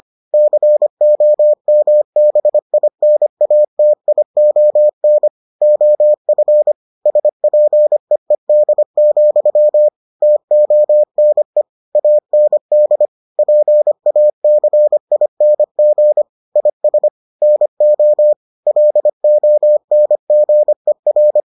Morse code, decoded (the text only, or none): COMBINATION OF SPEED, TONE AND PACING IS NO LONGER